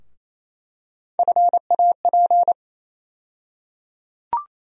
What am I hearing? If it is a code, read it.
Morse code, heard as a keyed tone, about 28 wpm, 700 Hz. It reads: FAP
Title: USA Call Sign Postfixes - 3 Letters 28wpm